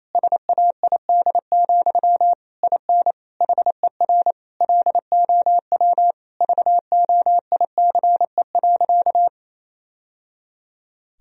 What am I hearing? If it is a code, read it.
Morse code, heard as a keyed tone, about 27 wpm, 700 Hz. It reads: SAID, IN HER LOW VOICE.
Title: Sentences from Top 1000 Words - Review Part 1 - 28wpm